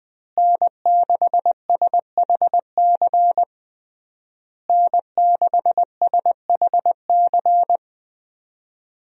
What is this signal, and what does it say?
Morse code, heard as a keyed tone, about 21 wpm, 700 Hz. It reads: N6SHC N6SHC
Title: USA Call Signs - 5 Characters - Repeated 3x - 20wpm